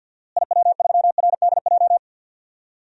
Morse code, heard as a keyed tone, about 46 wpm, 700 Hz. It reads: IW3RDJ
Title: DXCC-100 Call Signs - 6 Characters - Repeated 2x - 50wpm